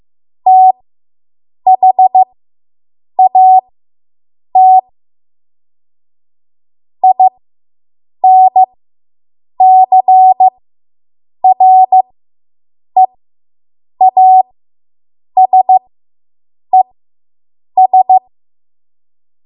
Morse code, heard as a keyed tone, about 15 wpm, spaced as for 7 wpm, 750 Hz. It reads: THAT INCREASES